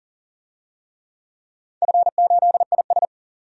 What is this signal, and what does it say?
Morse code, heard as a keyed tone, about 42 wpm, 700 Hz. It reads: F8IS
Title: DXCC-100 Call Signs - 4 Characters - 40wpm